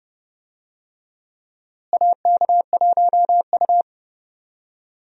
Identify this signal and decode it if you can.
Morse code, heard as a keyed tone, about 32 wpm, 700 Hz. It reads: AK1U
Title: USA Call Signs - 4 Characters - Repeated 2x - 30wpm